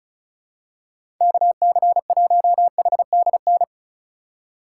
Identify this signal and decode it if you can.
Morse code, heard as a keyed tone, about 34 wpm, 700 Hz. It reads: KC1HDN
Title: USA Call Signs - 6 Characters - 2x Character Spacing - 35wpm